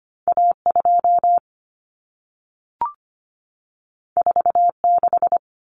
Morse code, heard as a keyed tone, about 25 wpm, 700 Hz. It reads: A2 46
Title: Common Cut Numbers - 2-Digit 25wpm